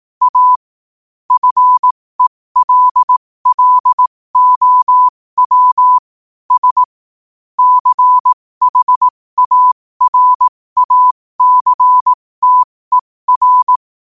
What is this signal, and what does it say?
Morse code, heard as a keyed tone, about 18 wpm, spaced as for 15 wpm, 1000 Hz. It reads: A FELLOW S CHARACTER
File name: qotdfiles / 15wpm